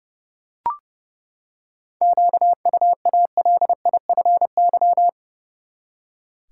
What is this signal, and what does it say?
Morse code, heard as a keyed tone, about 30 wpm, 700 Hz. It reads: QUALIFY
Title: Seven Letter Words - 30wpm